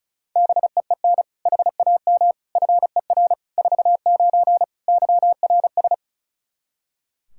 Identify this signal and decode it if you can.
Morse code, heard as a keyed tone, about 36 wpm, 700 Hz. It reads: BEEN HAM FER 49 YRS
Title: Practice QSO - Exchange 3 - More Info - 35wpm